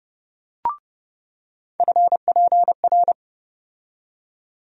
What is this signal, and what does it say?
Morse code, heard as a keyed tone, about 30 wpm, 700 Hz. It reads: FPR
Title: USA Call Sign Postfixes - 3 Letters 30wpm